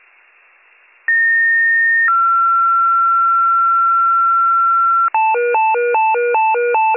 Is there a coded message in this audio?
Yes: two-tone page from 1.1 to 5.1 s, heard as a one-second tone, then a three-second tone, about 1850 then 1360 Hz.